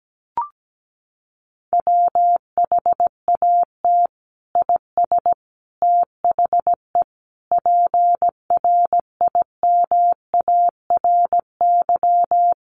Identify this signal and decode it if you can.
Morse code, heard as a keyed tone, about 17 wpm, 700 Hz. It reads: WHAT IS THE PRIMARY
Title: Sentences from Top 1100 Words 17wpm